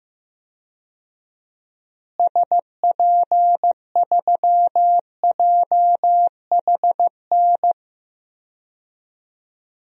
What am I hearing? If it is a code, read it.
Morse code, heard as a keyed tone, about 15 wpm, 700 Hz. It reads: SP3JHN
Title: DXCC-100 Call Signs - 6 Characters - Repeated 2x - 15wpm